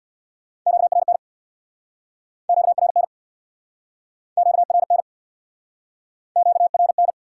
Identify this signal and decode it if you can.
Morse code, heard as a keyed tone, about 69 wpm, 700 Hz. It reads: QRN QRN QRN QRN